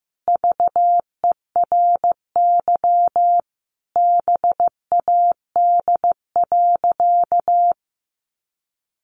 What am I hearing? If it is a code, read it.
Morse code, heard as a keyed tone, about 15 wpm, 700 Hz. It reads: VERY BAD.
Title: Sentences from Top 700 Words 15wpm